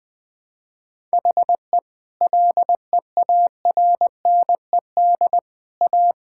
Morse code, heard as a keyed tone, about 21 wpm, 700 Hz. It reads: HE LEARNED A
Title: Sentences with 3-Word Phrases - 20wpm